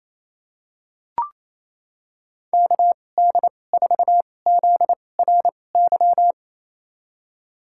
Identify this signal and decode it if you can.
Morse code, heard as a keyed tone, about 28 wpm, spaced as for 20 wpm, 700 Hz. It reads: KD4ZRY
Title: USA Call Signs - 6 Characters - 2x Character Spacing - 28wpm